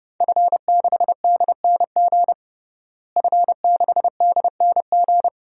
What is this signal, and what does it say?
Morse code, heard as a keyed tone, about 32 wpm, 700 Hz. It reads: F6DNG F6DNG
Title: DXCC-100 Call Signs - 5 Characters - Repeated 3x - 30wpm